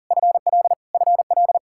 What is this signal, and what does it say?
Morse code, heard as a keyed tone, about 40 wpm, 700 Hz. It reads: FL FL
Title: Practice QSO - Exchange 1 - Beginning - 40wpm